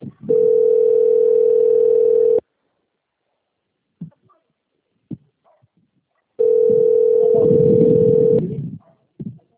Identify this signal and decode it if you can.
Ringback tone (North America), 440 + 480 Hz.
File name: out- - - - - .40644